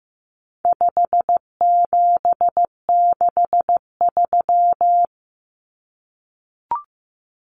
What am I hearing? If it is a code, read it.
Morse code, heard as a keyed tone, about 15 wpm, 700 Hz. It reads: 5763
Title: Numbers - 4-Digit 15wpm